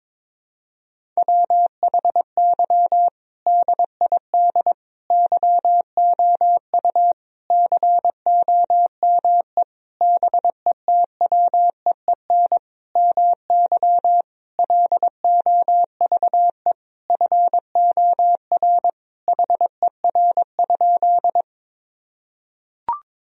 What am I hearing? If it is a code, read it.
Morse code, heard as a keyed tone, about 22 wpm, 700 Hz. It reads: WHY DID YOU COME BETWEEN MY LOVE FOR HER?
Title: Sentences from Top 1000 Words - Review Part 1 - 22wpm